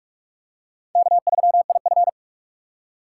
Morse code, heard as a keyed tone, about 40 wpm, 700 Hz. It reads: K3IF